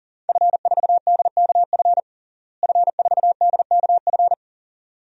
Morse code, heard as a keyed tone, about 40 wpm, 700 Hz. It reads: F4DKF F4DKF
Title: DXCC-100 Call Signs - 5 Characters - Repeated 2x - 40wpm